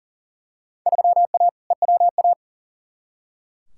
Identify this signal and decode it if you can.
Morse code, heard as a keyed tone, about 39 wpm, 700 Hz. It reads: 3A EWA
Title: Field Day Exchanges - 40wpm